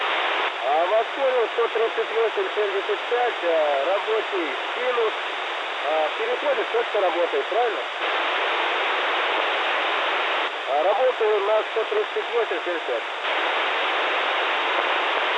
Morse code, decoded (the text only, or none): none